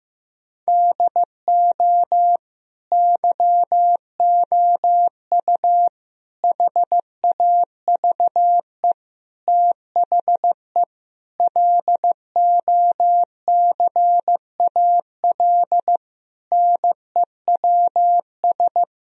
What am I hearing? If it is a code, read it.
Morse code, heard as a keyed tone, about 15 wpm, 700 Hz. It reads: DO YOU HAVE THE LOCAL NEWS
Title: Sentences from Top 1000 Words - Review Part 3 - 15wpm